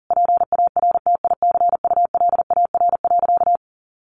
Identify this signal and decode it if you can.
Morse code, heard as a keyed tone, about 40 wpm, 700 Hz. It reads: PARTICULAR.
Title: Sentences from Top 600 Words 40wpm